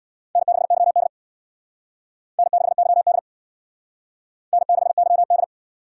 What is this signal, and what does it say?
Morse code, heard as a keyed tone, about 60 wpm, 700 Hz. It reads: N6YD N6YD N6YD
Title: Speed Racing - USA 1x2 Call Signs 50wpm